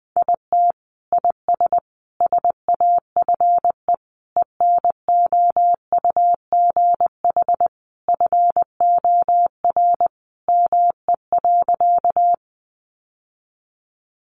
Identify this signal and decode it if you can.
Morse code, heard as a keyed tone, about 20 wpm, 700 Hz. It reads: IT IS SAFE ENOUGH FOR ME.
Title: Sentences from Top 900 Words 20wpm